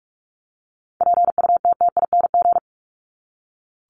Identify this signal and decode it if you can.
Morse code, heard as a keyed tone, about 44 wpm, 700 Hz. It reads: PUTTING
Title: ING Suffix - Top 100 Words with Base Words 45wpm